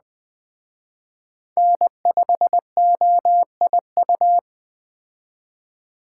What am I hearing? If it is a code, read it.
Morse code, heard as a keyed tone, about 20 wpm, 700 Hz. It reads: N5OIU
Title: Speed Racing - USA 1x3 Call Signs 17wpm